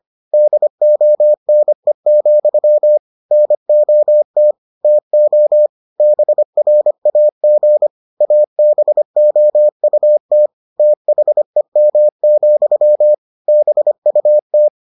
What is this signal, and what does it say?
Morse code, heard as a keyed tone, about 25 wpm, 600 Hz. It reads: DONE, NOT TO BRAG ABOUT THEM, BUT